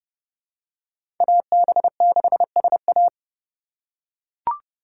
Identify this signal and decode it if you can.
Morse code, heard as a keyed tone, about 30 wpm, 700 Hz. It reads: AB6SA
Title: USA Call Signs - 5 Characters - 2x Character Spacing - 30wpm